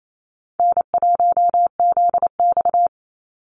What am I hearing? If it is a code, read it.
Morse code, heard as a keyed tone, about 27 wpm, 700 Hz. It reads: N1ZX
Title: Easy Call Signs - 28wpm